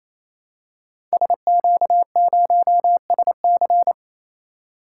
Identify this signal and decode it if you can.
Morse code, heard as a keyed tone, about 29 wpm, 700 Hz. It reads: SQ0SC
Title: Code Groups - 5 Letter:Numbers - 28wpm